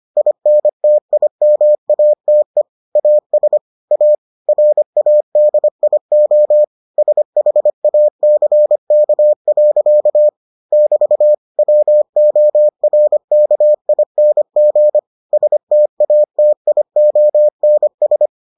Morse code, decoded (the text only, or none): INTIMATE AS A RADIO SHACK. <BT> WORKING STATIONS